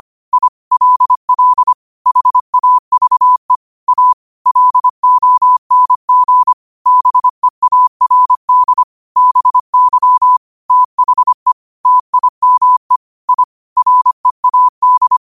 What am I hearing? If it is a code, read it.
Morse code, heard as a keyed tone, about 25 wpm, 1000 Hz. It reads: I LL HAVE A LONG BEARD BY THE TIME I READ
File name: qotdfiles / 25wpm